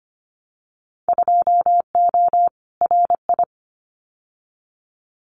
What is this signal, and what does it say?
Morse code, heard as a keyed tone, about 25 wpm, 700 Hz. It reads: 2O RI
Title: Winter Field Day Exchanges 25wpm